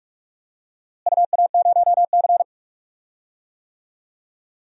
Morse code, heard as a keyed tone, about 45 wpm, 700 Hz. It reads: UA0C
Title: DXCC-100 Call Signs - 4 Characters - Repeated 2x - 45wpm